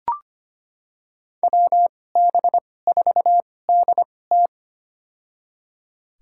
Morse code, heard as a keyed tone, about 25 wpm, spaced as for 18 wpm, 700 Hz. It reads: WB4DT
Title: USA Call Signs - 5 Characters - 2x Character Spacing - 25wpm